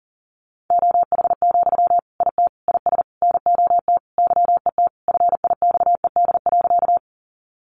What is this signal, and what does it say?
Morse code, heard as a keyed tone, about 37 wpm, 700 Hz. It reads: OH, IT IS NOT YET FIXED.
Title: Sentences from Top 1100 Words 40wpm